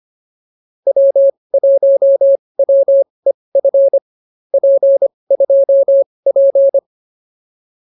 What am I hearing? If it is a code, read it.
Morse code, heard as a keyed tone, about 24 wpm, spaced as for 20 wpm, 550 Hz. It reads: W1WEF P2P